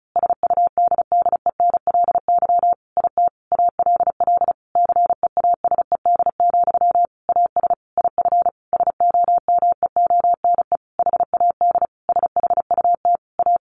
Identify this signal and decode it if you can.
Morse code, heard as a keyed tone, about 34 wpm, 700 Hz. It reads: SUDDENLY IT ALL CEASED, AS IF SOMEONE HAD SHUT A
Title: A Sound Of Thunder 35wpm